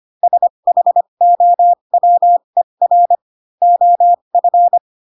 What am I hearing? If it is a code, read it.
Morse code, heard as a keyed tone, about 25 wpm, spaced as for 22 wpm, 700 Hz. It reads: SHOWER OF
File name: Pinocchio / Pinocchio ch6 18 25 ik1hge